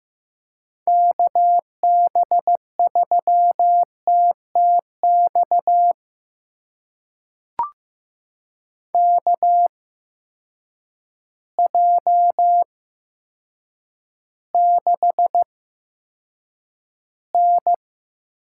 Morse code, heard as a keyed tone, about 15 wpm, spaced as for 8 wpm, 700 Hz. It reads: KB3TTX KJ6N
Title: USA Call Signs - 6 Characters - 8x Character Spacing - 15wpm